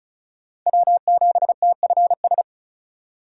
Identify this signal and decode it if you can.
Morse code, heard as a keyed tone, about 34 wpm, 700 Hz. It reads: W7TFS
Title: USA Call Signs - 5 Characters - 4x Character Spacing - 35wpm